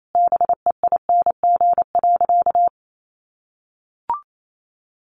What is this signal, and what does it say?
Morse code, heard as a keyed tone, about 28 wpm, 700 Hz. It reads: BEING.
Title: Sentences from Top 1500 Words 28wpm